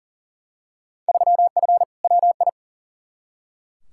Morse code, heard as a keyed tone, about 40 wpm, 700 Hz. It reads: 3F WI
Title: Field Day Exchanges - 40wpm